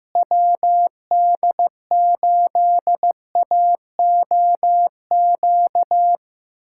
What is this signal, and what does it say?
Morse code, heard as a keyed tone, about 16 wpm, 700 Hz. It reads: WD8AOQ
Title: USA Callsign DE Callsign - 6 Characters - 15wpm